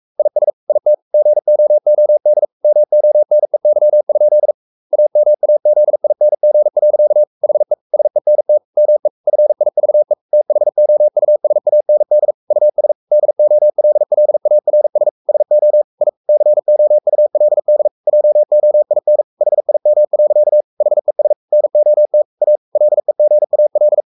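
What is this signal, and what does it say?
Morse code, heard as a keyed tone, about 44 wpm, 600 Hz. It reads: IS IT GOOD MONEY? AMAZING. HE SENT ME FIVE THOUSAND US DOLLARS SO I COULD JOIN HIM. HES NOT A LEGAL
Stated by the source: Morse code at 43 wpm sending Ken Follett's novel Never in English, from CW med Gnister.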